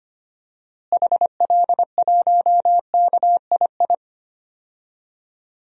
Morse code, heard as a keyed tone, about 25 wpm, 700 Hz. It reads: HL1KII
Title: DXCC-100 Call Signs - 6 Characters - Repeated 2x - 25wpm